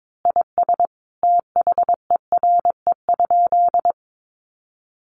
Morse code, heard as a keyed tone, about 22 wpm, 700 Hz. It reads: IS THERE?
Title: Sentences from Top 1500 Words 22wpm